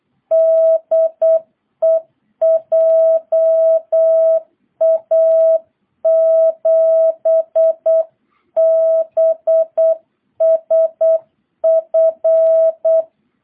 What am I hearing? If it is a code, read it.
Morse code, heard as a keyed tone, about 8 wpm, 660 Hz. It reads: DE JA7BSF